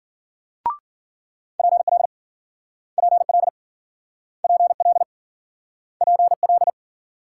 Morse code, heard as a keyed tone, about 52 wpm, 700 Hz. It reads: PL PL PL PL